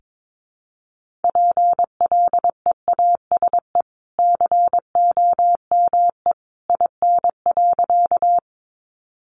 Morse code, heard as a keyed tone, about 22 wpm, 700 Hz. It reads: PLEASE COME IN.
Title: Sentences from Top 1500 Words 22wpm